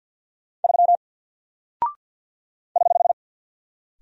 Morse code, heard as a keyed tone, about 48 wpm, 700 Hz. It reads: 3 <HH>
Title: Single Character (Letter:Number:Punctuation:Prosign) Alt 50wpm